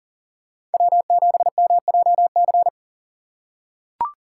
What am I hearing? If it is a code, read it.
Morse code, heard as a keyed tone, about 40 wpm, 700 Hz. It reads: W7MJC
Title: USA Call Signs - 6x Character Spacing - 40wpm